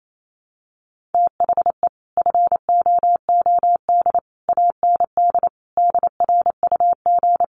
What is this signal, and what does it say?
Morse code, heard as a keyed tone, about 28 wpm, 700 Hz. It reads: THE FOOD AND DRUG
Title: Sets of 5 Words 28wpm